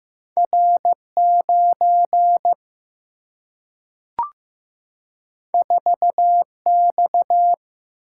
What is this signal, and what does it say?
Morse code, heard as a keyed tone, about 15 wpm, 700 Hz. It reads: R 9 4 X
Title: Single-Letter Single-Number 15wpm